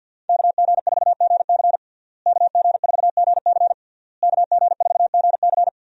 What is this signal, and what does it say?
Morse code, heard as a keyed tone, about 51 wpm, 700 Hz. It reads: KG4GC KG4GC KG4GC
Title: USA Call Signs - 5 Characters - Repeated 3x - 50wpm